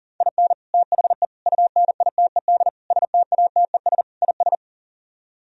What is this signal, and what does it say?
Morse code, heard as a keyed tone, about 50 wpm, 700 Hz. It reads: IN THE UNITED STATES IS
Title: Speed-Racing - Sets of 5 Words 40wpm